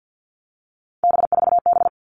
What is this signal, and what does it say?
Morse code, heard as a keyed tone, about 50 wpm, 700 Hz. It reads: B4B
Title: All Cut Numbers - 3-Digit 50wpm